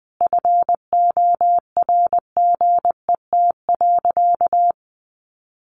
Morse code, heard as a keyed tone, about 20 wpm, 700 Hz. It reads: FORGET.
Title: Sentences from Top 1200 Words 20wpm